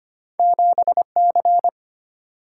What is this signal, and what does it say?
Morse code, heard as a keyed tone, about 25 wpm, 700 Hz. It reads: 7C
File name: Two Letter-Number Combinations - 25wpm